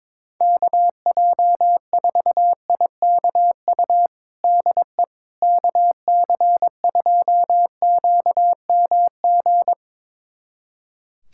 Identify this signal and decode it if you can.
Morse code, heard as a keyed tone, about 22 wpm, 700 Hz. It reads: KJ4IKU DE KC2QMG
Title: USA Callsign DE Callsign - 6 Characters - 22wpm